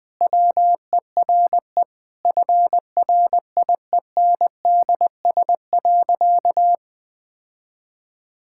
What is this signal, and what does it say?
Morse code, heard as a keyed tone, about 20 wpm, 700 Hz. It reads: WERE FRIENDS.
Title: Sentences with 2-Word Phrases - Spoken Beginning - 20wpm